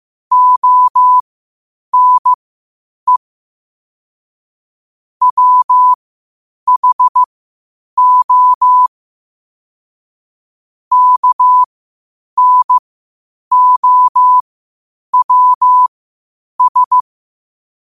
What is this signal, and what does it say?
Morse code, heard as a keyed tone, about 15 wpm, spaced as for 8 wpm, 1000 Hz. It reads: ONE WHO KNOWS
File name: qotdfiles / 8wpm